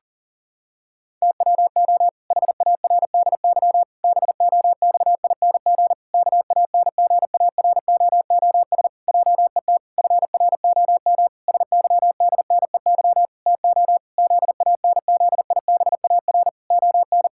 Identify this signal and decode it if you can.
Morse code, heard as a keyed tone, about 40 wpm, 700 Hz. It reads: TWO HARDY BOXING KANGAROOS JET FROM SYDNEY TO ZANZIBAR ON
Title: Pangram Challenge v2 - 40wpm